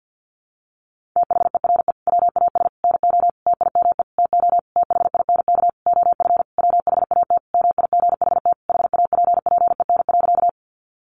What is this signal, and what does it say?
Morse code, heard as a keyed tone, about 50 wpm, 700 Hz. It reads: THERE WAS NO TIME TO THINK OF WHAT MIGHT HAPPEN.
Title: Sentences from Top 400 Words 50wpm